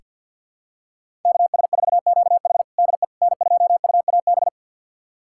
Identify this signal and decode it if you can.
Morse code, heard as a keyed tone, about 48 wpm, 700 Hz. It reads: KS4QH DE N2UAB
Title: USA Callsign DE Callsign - 5 Characters - 50wpm